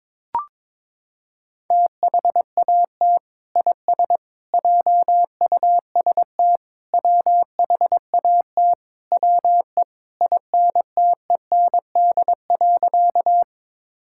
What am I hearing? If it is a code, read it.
Morse code, heard as a keyed tone, about 22 wpm, 700 Hz. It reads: THAT IS JUST WHAT WE INTEND.
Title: Sentences from Top 1600 Words 22wpm